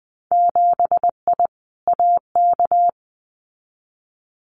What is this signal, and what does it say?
Morse code, heard as a keyed tone, about 20 wpm, 700 Hz. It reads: 7I AK
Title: Winter Field Day Exchanges Encore 20wpm